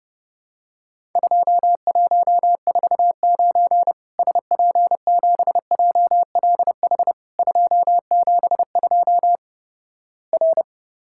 Morse code, heard as a keyed tone, about 30 wpm, 700 Hz. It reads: 2149 SP7JLH 272 R
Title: Worked All Europe DX Contest - QTC Exchange - 30wpm